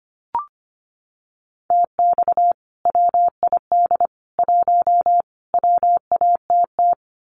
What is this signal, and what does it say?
Morse code, heard as a keyed tone, about 25 wpm, 700 Hz. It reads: TX WID 1 WATT
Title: Common QSO Phrases 25wpm